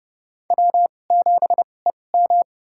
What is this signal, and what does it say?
Morse code, heard as a keyed tone, about 30 wpm, spaced as for 22 wpm, 700 Hz. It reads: W7EM
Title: USA Call Signs - 2x Character Spacing - 30wpm